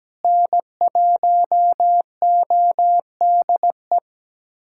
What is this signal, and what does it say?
Morse code, heard as a keyed tone, about 17 wpm, 700 Hz. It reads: N1ODE
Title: USA Callsign DE Callsign - 5 Characters - 17wpm